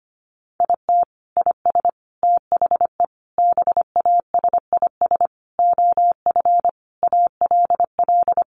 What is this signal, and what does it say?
Morse code, heard as a keyed tone, about 25 wpm, 700 Hz. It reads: IT IS THE BASIS OF ALL
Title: Sentences from Top 1200 Words 25wpm